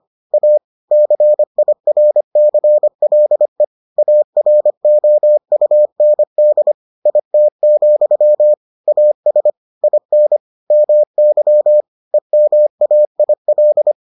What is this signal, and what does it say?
Morse code, heard as a keyed tone, about 25 wpm, 600 Hz. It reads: A CIRCLE AROUND IT, AS IN MY EMAIL